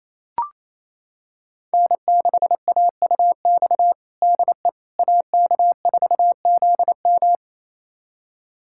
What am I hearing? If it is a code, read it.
Morse code, heard as a keyed tone, about 27 wpm, 700 Hz. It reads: N6AUX DE AK4ZM
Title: USA Callsign DE Callsign - 5 Characters - 28wpm